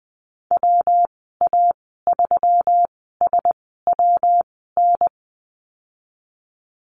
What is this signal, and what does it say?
Morse code, heard as a keyed tone, about 20 wpm, spaced as for 15 wpm, 700 Hz. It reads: WA3SWN
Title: 179 Lesson 29g - Slash - Call Signs 20wpm